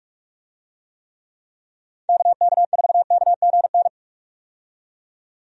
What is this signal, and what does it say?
Morse code, heard as a keyed tone, about 46 wpm, 700 Hz. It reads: KK4KGN
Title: USA Call Signs - 6 Characters - 45wpm